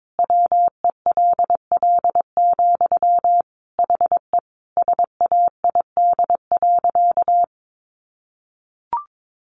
Morse code, heard as a keyed tone, about 22 wpm, 700 Hz. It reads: WELL, HE SAID.
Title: Sentences from Top 700 Words 22wpm